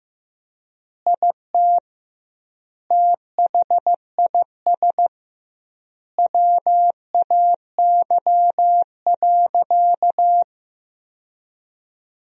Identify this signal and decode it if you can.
Morse code, heard as a keyed tone, about 15 wpm, 700 Hz. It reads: IT THIS WAY.
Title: 2x Word-Spacing - Sentences from Top 100 Words - 15wpm